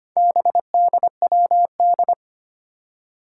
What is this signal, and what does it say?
Morse code, heard as a keyed tone, about 25 wpm, 700 Hz. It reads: BDWD
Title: Four Challenging Letter-Number Combinations 25wpm